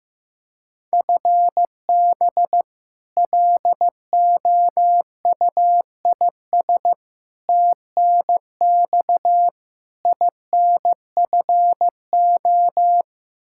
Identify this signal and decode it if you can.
Morse code, heard as a keyed tone, about 15 wpm, 700 Hz. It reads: FB LOUIS TNX INFO
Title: Practice QSO - Exchange 3 - More Info - 15wpm